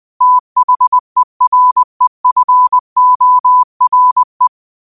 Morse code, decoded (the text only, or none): THEREFORE